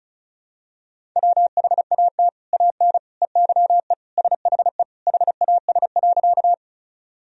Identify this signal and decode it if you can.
Morse code, heard as a keyed tone, about 35 wpm, 700 Hz. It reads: WHAT AN EYE SHE HAS.
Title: Sentences from Top 1000 Words - Review Part 1 - 35wpm